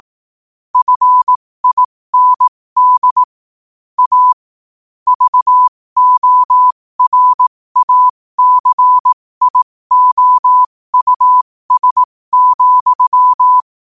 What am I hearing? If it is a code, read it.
Morse code, heard as a keyed tone, about 18 wpm, spaced as for 15 wpm, 1000 Hz. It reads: FIND A VORACIOUS,
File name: qotdfiles / 15wpm